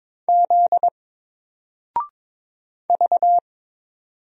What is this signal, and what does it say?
Morse code, heard as a keyed tone, about 22 wpm, 700 Hz. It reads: Z V
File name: Single-Letter-Number-22wpm